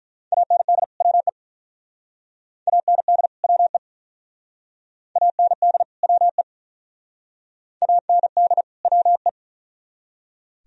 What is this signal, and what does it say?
Morse code, heard as a keyed tone, about 42 wpm, 700 Hz. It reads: AND WE AND WE AND WE AND WE
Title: Speed-Racing - Sets of 2 Words 35wpm